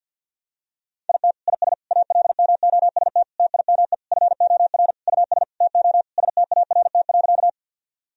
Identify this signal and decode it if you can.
Morse code, heard as a keyed tone, about 50 wpm, 700 Hz. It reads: IT IS ALMOST TIME FOR US TO START.
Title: Sentences from Top 1000 Words - Review Part 4 - 50wpm